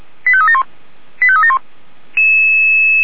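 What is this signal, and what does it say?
Five-tone selective call (ZVEI), heard as quick runs of five tones, tones of about 70 ms. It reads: 85371 85371
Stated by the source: ZVEI five-tone call received by SDR.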